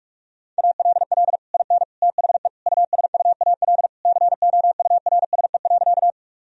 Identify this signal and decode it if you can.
Morse code, heard as a keyed tone, about 43 wpm, 700 Hz. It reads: ALL IN THE USUAL COURSE.
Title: Sentences from Top 1000 Words - Review Part 2 - 45wpm